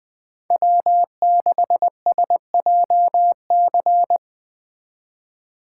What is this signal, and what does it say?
Morse code, heard as a keyed tone, about 20 wpm, 700 Hz. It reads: W6SJC
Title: Speed Racing - USA Call Signs 20wpm